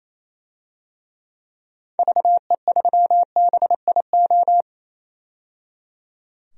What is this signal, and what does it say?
Morse code, heard as a keyed tone, about 28 wpm, 700 Hz. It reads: VE3BIO
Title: DXCC-100 Call Signs - 6 Characters - 28wpm